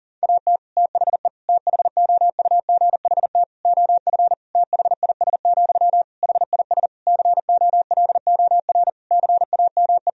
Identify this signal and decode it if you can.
Morse code, heard as a keyed tone, about 37 wpm, 700 Hz. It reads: AT THE THOUGHT OF THIS, HIS COLOR CAME
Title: Sentences from Top 1000 Words - Review Part 2 - 40wpm